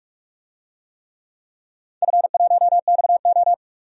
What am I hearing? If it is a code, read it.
Morse code, heard as a keyed tone, about 48 wpm, 700 Hz. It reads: F1XO
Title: DXCC-100 Call Signs - 4 Characters - 45wpm